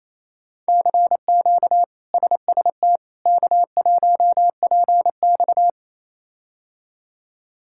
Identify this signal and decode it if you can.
Morse code, heard as a keyed tone, about 28 wpm, 700 Hz. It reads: CQ SST K1PX
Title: K1USN SST QSO Practice - 28wpm